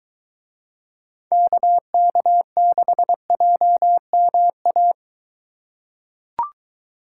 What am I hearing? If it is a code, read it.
Morse code, heard as a keyed tone, about 23 wpm, 700 Hz. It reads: KK6JMA